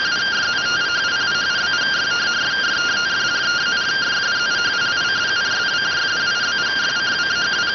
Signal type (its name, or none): two-tone teleprinter signal